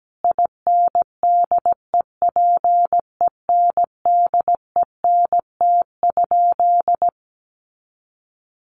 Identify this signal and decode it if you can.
Morse code, heard as a keyed tone, about 17 wpm, 700 Hz. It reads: INDEPENDENT?
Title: Sentences from Top 1400 Words 17wpm